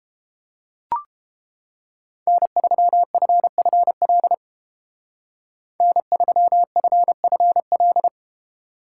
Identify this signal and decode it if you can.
Morse code, heard as a keyed tone, about 30 wpm, 700 Hz. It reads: N3FFL N3FFL